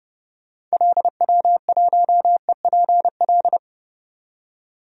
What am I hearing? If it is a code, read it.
Morse code, heard as a keyed tone, about 30 wpm, 700 Hz. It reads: LW1EPL
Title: DXCC-100 Call Signs - 6 Characters - 30wpm